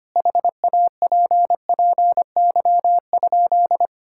Morse code, heard as a keyed tone, about 25 wpm, 700 Hz. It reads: HAPPY?
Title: Sentences from Top 1000 Words - Review Part 6 - 25wpm